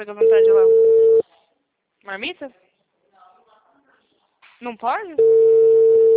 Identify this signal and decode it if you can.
Ringback tone (China, 450 Hz), with speech.